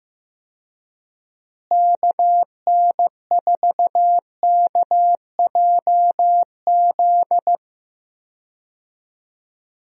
Morse code, heard as a keyed tone, about 15 wpm, 700 Hz. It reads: KN4KJZ
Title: USA Call Signs - 6 Characters - Repeated 2x - 15wpm